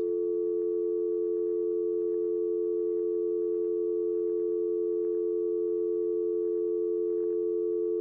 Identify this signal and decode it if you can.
Dial tone (North America), 350 + 440 Hz.
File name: functions / downloadSound